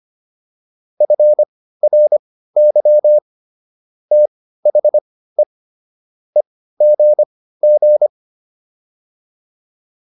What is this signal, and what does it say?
Morse code, heard as a keyed tone, about 25 wpm, spaced as for 15 wpm, 600 Hz. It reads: FRY THE EGG